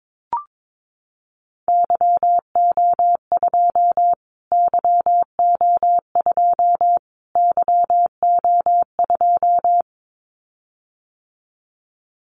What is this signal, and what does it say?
Morse code, heard as a keyed tone, about 22 wpm, 700 Hz. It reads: YO2 YO2 YO2
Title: Callsign Prefixes for Top 25 DXCC - Repeated 3x - 22wpm